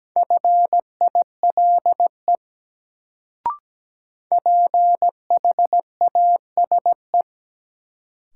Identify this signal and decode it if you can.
Morse code, heard as a keyed tone, about 17 wpm, 700 Hz. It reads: FILE PHASE
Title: Sentences from Top 1200 Words - New Word Drill Alt - 17wpm